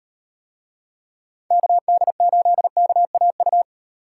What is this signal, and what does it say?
Morse code, heard as a keyed tone, about 38 wpm, 700 Hz. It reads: KD8KAU